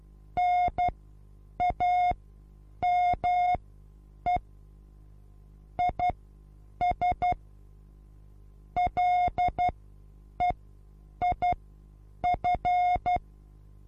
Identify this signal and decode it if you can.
Morse code, heard as a keyed tone, about 12 wpm, spaced as for 8 wpm, 720 Hz. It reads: NAME IS LEIF
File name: Onlinelehrgang / morsen / DJ4UF Morsen23